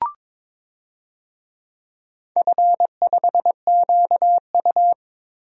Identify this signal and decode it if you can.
Morse code, heard as a keyed tone, about 22 wpm, 690 Hz. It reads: F5QU
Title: DXCC-100 Call Signs - 4 Characters - 22wpm